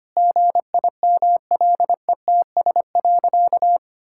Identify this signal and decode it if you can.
Morse code, heard as a keyed tone, about 25 wpm, 700 Hz. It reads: GIMLETS.
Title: Pangram Challenge v2 - 25wpm